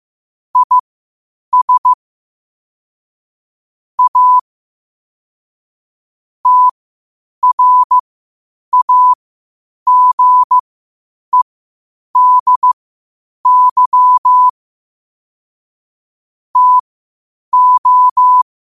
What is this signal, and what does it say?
Morse code, heard as a keyed tone, about 15 wpm, spaced as for 8 wpm, 1000 Hz. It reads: IS A TRAGEDY TO
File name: qotdfiles / 8wpm